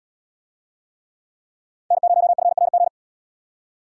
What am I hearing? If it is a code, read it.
Morse code, heard as a keyed tone, about 67 wpm, 700 Hz. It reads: N0FRG